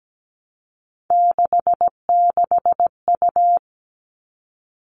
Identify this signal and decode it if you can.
Morse code, heard as a keyed tone, about 17 wpm, 700 Hz. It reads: 66U
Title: All Cut Numbers - 3-Digit 17wpm